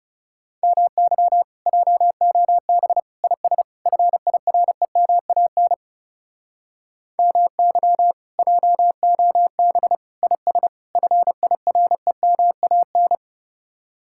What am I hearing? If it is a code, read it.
Morse code, heard as a keyed tone, about 37 wpm, 700 Hz. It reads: MY JOB IS FIREMAN MY JOB IS FIREMAN
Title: Speed-Racing Common QSO Phrases - 30wpm